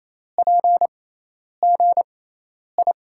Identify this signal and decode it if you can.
Morse code, heard as a keyed tone, about 28 wpm, spaced as for 10 wpm, 700 Hz. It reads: PGI
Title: USA Call Signs - 6 Characters - 6x Character Spacing - 28wpm